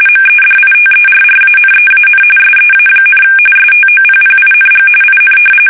RTTY, about 45.45 baud. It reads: MEDITERRANEAN SEA, STOPPING IN
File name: fsk850 / battleshipnj